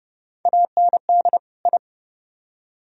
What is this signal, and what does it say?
Morse code, heard as a keyed tone, about 35 wpm, 700 Hz. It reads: AND I